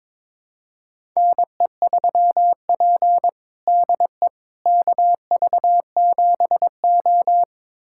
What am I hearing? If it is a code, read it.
Morse code, heard as a keyed tone, about 22 wpm, 700 Hz. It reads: NE3P DE KV7O
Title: USA Callsign DE Callsign - 4 Characters - 22wpm